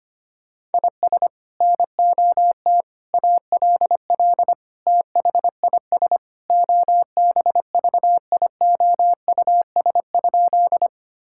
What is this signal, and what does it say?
Morse code, heard as a keyed tone, about 25 wpm, 700 Hz. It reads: IS NOT ALL THIS OBVIOUS?
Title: Sentences from Top 1600 Words 25wpm